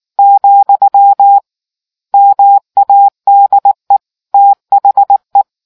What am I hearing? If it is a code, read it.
Morse code, heard as a keyed tone, about 19 wpm, 800 Hz. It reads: , MADE THE